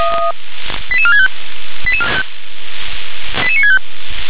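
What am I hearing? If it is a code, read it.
Five-tone selective call (ZVEI), heard as quick runs of five tones, tones of about 70 ms. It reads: 88465 88965
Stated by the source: ZVEI five-tone selective call received on SDR.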